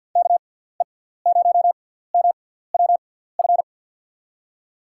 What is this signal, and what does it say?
Morse code, heard as a keyed tone, about 53 wpm, 700 Hz. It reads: K E 0 M W F
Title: USA Call Signs - 6 Characters - 6x Character Spacing - 50wpm